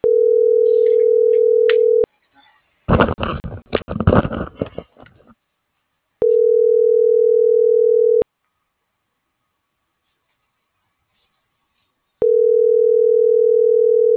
Ringback tone (North America), 440 + 480 Hz.